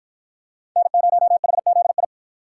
Morse code, heard as a keyed tone, about 50 wpm, 700 Hz. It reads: N0HZI